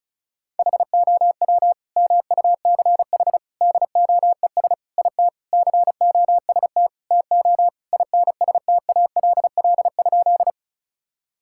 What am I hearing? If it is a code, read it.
Morse code, heard as a keyed tone, about 36 wpm, 700 Hz. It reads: HOW MUCH DOES IT COST TO INSTALL?
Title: Sentences from Top 1600 Words 35wpm